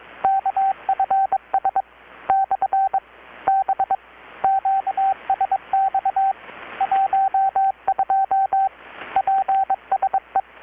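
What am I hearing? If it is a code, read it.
Morse code, heard as a keyed tone, about 23 wpm, 780 Hz, with radio noise. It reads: KFS / B QSX 12 PSE
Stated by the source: Morse from KPH's Night of Nights XX maritime radiotelegraphy event, received in Chiapas, Mexico about 3,800 km away on a Kenwood R-600 with a grounded 10 m long-wire antenna.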